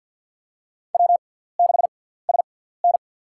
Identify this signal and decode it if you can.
Morse code, heard as a keyed tone, about 47 wpm, spaced as for 17 wpm, 700 Hz. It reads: W6SN
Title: USA Call Signs - 4 Characters - 6x Character Spacing - 50wpm